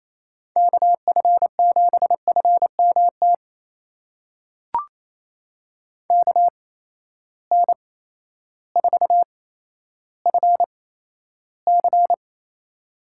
Morse code, heard as a keyed tone, about 28 wpm, spaced as for 13 wpm, 700 Hz. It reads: KF7FMT KN4FC
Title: USA Call Signs - 8x Character Spacing - 28wpm